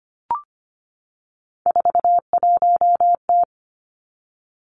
Morse code, heard as a keyed tone, about 25 wpm, 700 Hz. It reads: 41T
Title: Three Challenging Letter-Number Combinations 25wpm